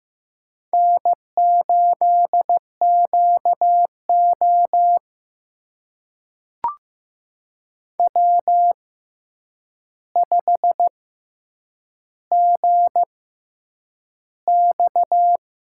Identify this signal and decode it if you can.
Morse code, heard as a keyed tone, about 15 wpm, spaced as for 8 wpm, 700 Hz. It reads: N8QO W5GX
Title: USA Call Signs - 4 Characters - 6x Character Spacing - 15wpm